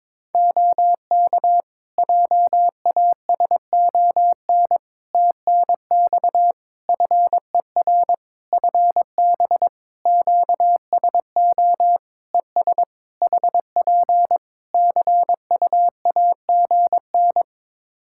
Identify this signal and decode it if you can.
Morse code, heard as a keyed tone, about 22 wpm, 700 Hz. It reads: OK JASON TNX FER FB QSO ES HP CUAGN
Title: Practice QSO - Exchange 4 - Ending - 22wpm